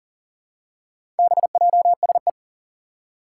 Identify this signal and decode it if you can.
Morse code, heard as a keyed tone, about 40 wpm, 700 Hz. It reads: BJSE
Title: Four Challenging Letter-Number Combinations 40wpm